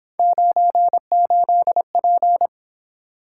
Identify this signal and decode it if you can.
Morse code, heard as a keyed tone, about 26 wpm, 700 Hz. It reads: 98P
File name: Speed-Racing - Three Letter-Number Combinations - 22wpm